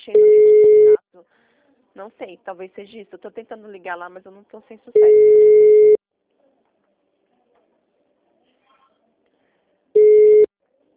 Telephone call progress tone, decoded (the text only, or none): ringback tone